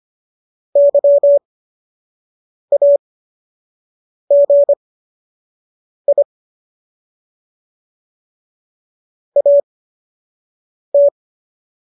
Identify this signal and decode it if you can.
Morse code, heard as a keyed tone, about 25 wpm, spaced as for 6 wpm, 570 Hz. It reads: YAGI AT
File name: qso002 06